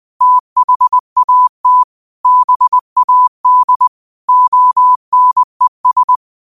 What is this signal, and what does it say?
Morse code, heard as a keyed tone, about 20 wpm, 1000 Hz. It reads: THAT BAD ONES
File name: qotdfiles / 20wpm